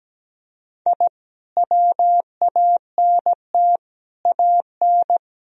Morse code, heard as a keyed tone, about 17 wpm, 700 Hz. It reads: I WANT AN
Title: Sentences from Top 1000 Words - Review Part 1 - 17wpm